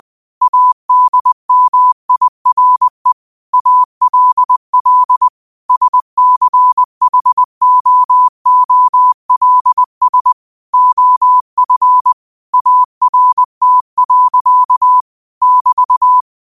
Morse code, heard as a keyed tone, about 20 wpm, 1000 Hz. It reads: ADMIRE ALL SCHOOLS OF ART. <BT>
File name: qotdfiles / 20wpm